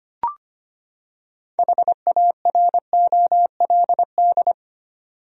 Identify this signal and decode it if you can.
Morse code, heard as a keyed tone, about 25 wpm, 700 Hz. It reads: HAROLD
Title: USA Common Names - 25wpm